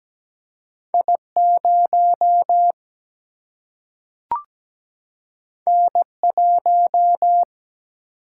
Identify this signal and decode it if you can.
Morse code, heard as a keyed tone, about 17 wpm, 700 Hz. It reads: I0 N1
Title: Single-Letter Single-Number 17wpm